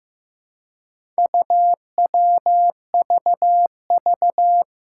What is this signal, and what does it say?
Morse code, heard as a keyed tone, about 15 wpm, 700 Hz. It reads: UWVV
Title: Four Challenging Letter-Number Combinations 15wpm